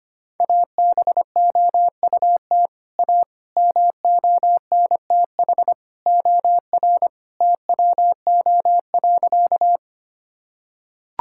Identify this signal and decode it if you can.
Morse code, heard as a keyed tone, about 25 wpm, 700 Hz. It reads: ABOUT A MONTH OR TWO.
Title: Sentences from Top 1000 Words - Review Part 5 - 25wpm